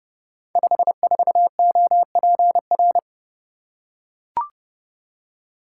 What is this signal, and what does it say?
Morse code, heard as a keyed tone, about 28 wpm, 700 Hz. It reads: 54OPR
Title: Code Groups - 5 Letter:Numbers - 30wpm